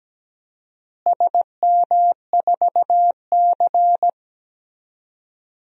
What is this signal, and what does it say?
Morse code, heard as a keyed tone, about 17 wpm, 700 Hz. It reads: SM4C
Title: DXCC-100 Call Signs - 4 Characters - Repeated 2x - 17wpm